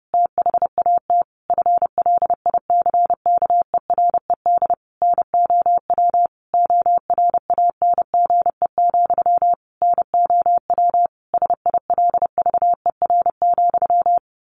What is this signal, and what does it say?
Morse code, heard as a keyed tone, about 30 wpm, 700 Hz. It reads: THAT FLICKERED NOW ORANGE, NOW SILVER,
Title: A Sound Of Thunder 30wpm